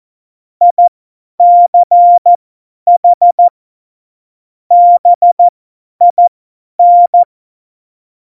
Morse code, heard as a keyed tone, about 14 wpm, spaced as for 10 wpm, 700 Hz. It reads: ICH BIN
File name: Morsekurs / Witze / Witz 12. Juni 2018, 50 F70